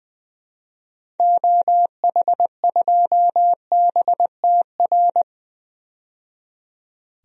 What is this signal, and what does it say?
Morse code, heard as a keyed tone, about 20 wpm, 700 Hz. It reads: OH2BTR
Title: DXCC-100 Call Signs - 6 Characters - 20wpm